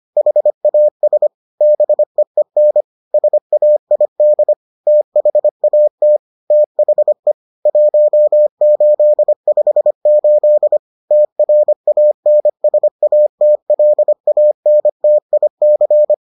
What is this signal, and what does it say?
Morse code, heard as a keyed tone, about 25 wpm, 600 Hz. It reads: HAS BEEN SAID THAT THE 1858 TRANSATLANTIC